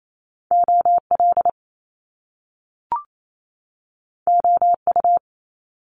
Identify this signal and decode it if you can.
Morse code, heard as a keyed tone, about 28 wpm, 700 Hz. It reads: O L O U
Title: Top Two Letter Combinations in Words 28wpm